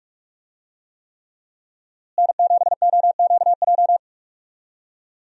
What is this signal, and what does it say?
Morse code, heard as a keyed tone, about 43 wpm, 700 Hz. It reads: N7OQJ
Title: USA Call Signs - 5 Characters - Repeated 3x - 45wpm